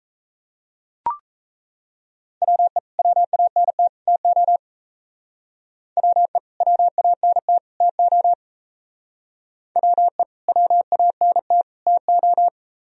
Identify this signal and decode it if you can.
Morse code, heard as a keyed tone, about 34 wpm, 700 Hz. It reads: WE WANT TO WE WANT TO WE WANT TO
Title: Speed-Racing - Sets of 3 Words 28wpm